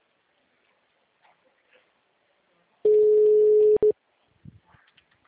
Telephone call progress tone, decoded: ringback tone

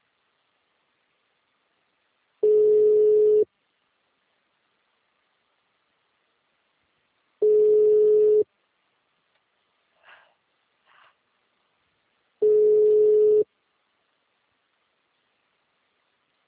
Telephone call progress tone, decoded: ringback tone

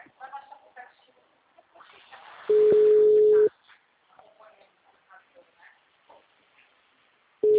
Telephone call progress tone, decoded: ringback tone

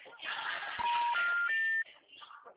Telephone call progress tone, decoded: special information tone

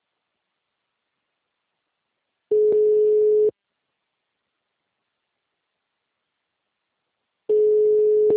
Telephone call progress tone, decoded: ringback tone

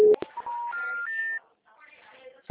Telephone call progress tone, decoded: special information tone

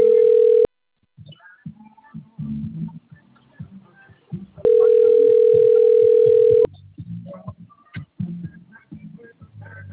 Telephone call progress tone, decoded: ringback tone

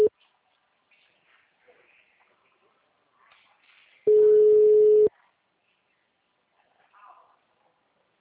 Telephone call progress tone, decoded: ringback tone

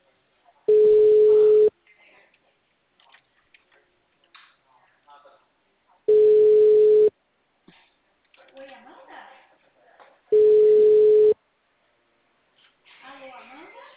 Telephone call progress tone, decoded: ringback tone

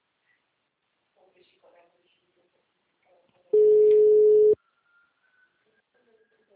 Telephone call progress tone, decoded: ringback tone

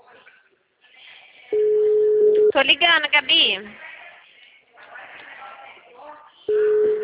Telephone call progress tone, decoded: ringback tone

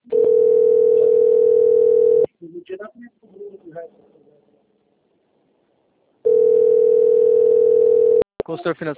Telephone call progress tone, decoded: ringback tone

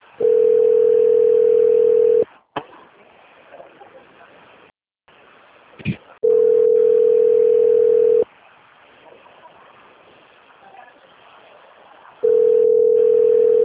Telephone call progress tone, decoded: ringback tone

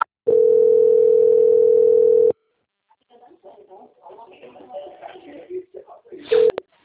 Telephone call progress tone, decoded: ringback tone

